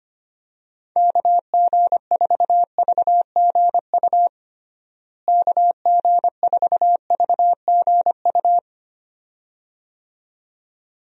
Morse code, heard as a keyed tone, about 25 wpm, 700 Hz. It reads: KG4VGU KG4VGU
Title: USA Call Signs - 6 Characters - Repeated 2x - 25wpm